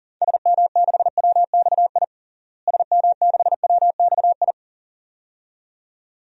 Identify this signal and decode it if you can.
Morse code, heard as a keyed tone, about 40 wpm, 700 Hz. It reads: SM6WXI SM6WXI
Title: DXCC-100 Call Signs - 6 Characters - Repeated 2x - 40wpm